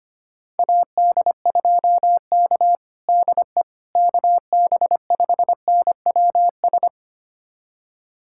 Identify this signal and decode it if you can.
Morse code, heard as a keyed tone, about 25 wpm, 700 Hz. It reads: AD2K DE KB5NWS
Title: USA Callsign DE Callsign 25wpm